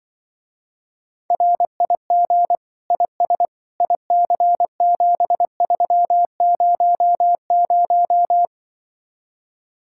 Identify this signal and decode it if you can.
Morse code, heard as a keyed tone, about 24 wpm, 700 Hz. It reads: RIG IS IC7300
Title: Speed-Racing Common QSO Phrases - 20wpm